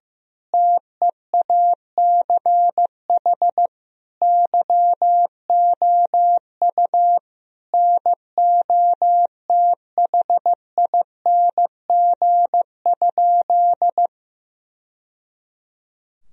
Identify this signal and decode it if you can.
Morse code, heard as a keyed tone, about 15 wpm, 700 Hz. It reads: TEACH YOU NOTHING?
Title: Sentences from Top 1000 Words - Review Part 2 - 15wpm